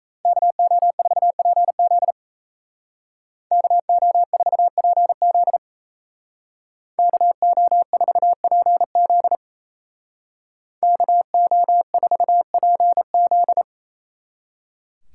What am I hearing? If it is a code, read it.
Morse code, heard as a keyed tone, about 33 wpm, 700 Hz. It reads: KO4PZ KO4PZ KO4PZ KO4PZ
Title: Speed Racing - USA 2x2 Call Signs 28wpm